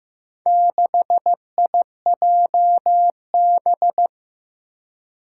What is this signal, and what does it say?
Morse code, heard as a keyed tone, about 15 wpm, 700 Hz. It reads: 6IJB
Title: Four Challenging Letter-Number Combinations 15wpm